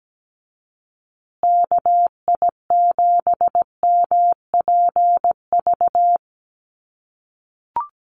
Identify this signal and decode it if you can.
Morse code, heard as a keyed tone, about 17 wpm, 700 Hz. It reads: KI7MPV
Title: USA Call Signs Repeated 3x 17wpm